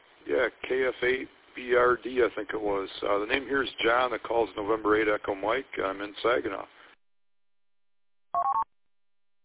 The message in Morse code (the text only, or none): none